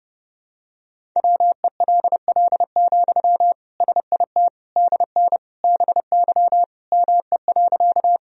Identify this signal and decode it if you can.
Morse code, heard as a keyed tone, about 30 wpm, 700 Hz. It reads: WELL, SIT DN BY ME.
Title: Sentences from Top 500 Words and Abbreviations 30wpm